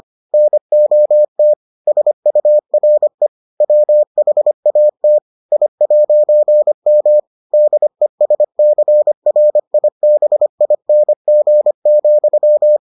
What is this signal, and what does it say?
Morse code, heard as a keyed tone, about 25 wpm, 600 Hz. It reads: NOT SURE WHAT I'M DESCRIBING,